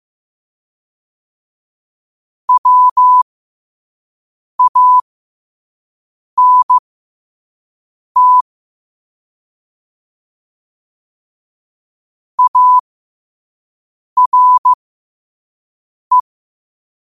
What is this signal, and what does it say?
Morse code, heard as a keyed tone, about 15 wpm, spaced as for 5 wpm, 1000 Hz. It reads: WANT ARE
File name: qotdfiles / 5wpm